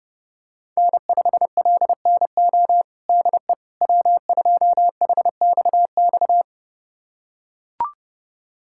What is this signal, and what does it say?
Morse code, heard as a keyed tone, about 30 wpm, 700 Hz. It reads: N5LNO DE W2HXX
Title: USA Callsign DE Callsign - 5 Characters - 30wpm